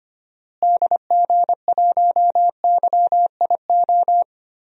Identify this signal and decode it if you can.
Morse code, heard as a keyed tone, about 25 wpm, 700 Hz. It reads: DG1YIO
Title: DXCC-100 Call Signs - 6 Characters - Repeated 2x - 25wpm